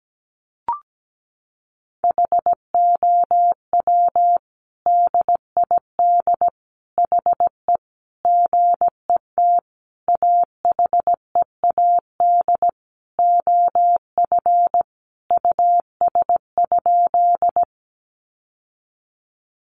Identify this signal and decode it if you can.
Morse code, heard as a keyed tone, about 17 wpm, 700 Hz. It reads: HOW DID HE GET AHEAD OF US?
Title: Sentences from Top 1100 Words 17wpm